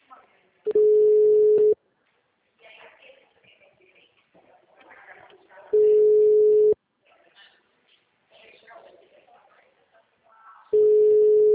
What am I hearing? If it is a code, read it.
Ringback tone (Europe, 425 Hz), with speech.
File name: out- -2011- - - .64